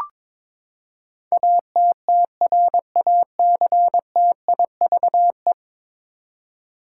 Morse code, heard as a keyed tone, about 22 wpm, 700 Hz. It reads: ATTRACTIVE
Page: morse-code-ninja-practice-22wpm